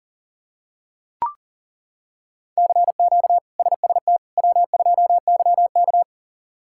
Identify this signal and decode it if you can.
Morse code, heard as a keyed tone, about 40 wpm, 700 Hz. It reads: CQ SST W2YK
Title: K1USN SST QSO Practice - 40wpm